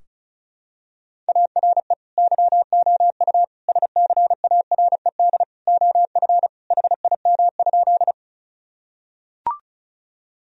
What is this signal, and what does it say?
Morse code, heard as a keyed tone, about 36 wpm, 700 Hz. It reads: ARE YOU SCARED OF HIM?
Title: Sentences from Top 1000 Words - Review Part 4 - 35wpm